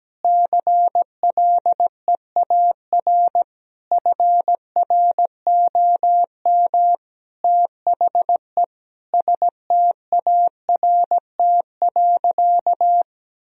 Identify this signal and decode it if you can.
Morse code, heard as a keyed tone, about 17 wpm, 700 Hz. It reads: CLEAR FROM THE START.
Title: Sentences from Top 1000 Words - Review Part 1 - 17wpm